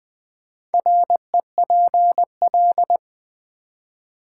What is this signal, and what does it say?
Morse code, heard as a keyed tone, about 20 wpm, 700 Hz. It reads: REPL
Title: Build A Word - 1 Letter At A Time - 3 to 5-Letter Words - 20wpm